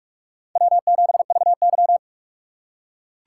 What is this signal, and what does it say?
Morse code, heard as a keyed tone, about 46 wpm, 700 Hz. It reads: W7VY
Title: USA Call Signs - 3x Character Spacing - 45wpm